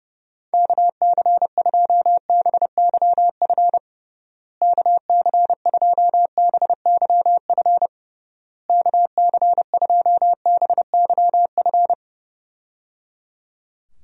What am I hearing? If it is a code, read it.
Morse code, heard as a keyed tone, about 32 wpm, 700 Hz. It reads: KC2BYF KC2BYF KC2BYF
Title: USA Call Signs - 6 Characters - Repeated 3x - 30wpm